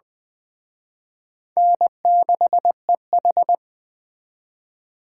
Morse code, heard as a keyed tone, about 20 wpm, 700 Hz. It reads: N6EH
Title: Speed Racing - USA 1x2 Call Signs 17wpm